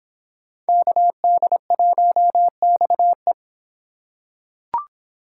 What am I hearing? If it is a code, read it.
Morse code, heard as a keyed tone, about 25 wpm, 700 Hz. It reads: KD1XE